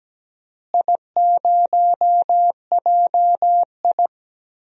Morse code, heard as a keyed tone, about 17 wpm, 700 Hz. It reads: I0JI
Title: DXCC-100 Call Signs - 4 Characters - Repeated 3x - 17wpm